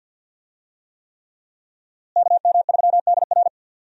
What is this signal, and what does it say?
Morse code, heard as a keyed tone, about 51 wpm, 700 Hz. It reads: KM3DR
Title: USA Call Signs - 5 Characters - Repeated 3x - 50wpm